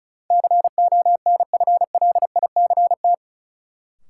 Morse code, heard as a keyed tone, about 36 wpm, 700 Hz. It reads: CONFLICT
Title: Sentences from Top 1300 Words - New Word Drill - 35wpm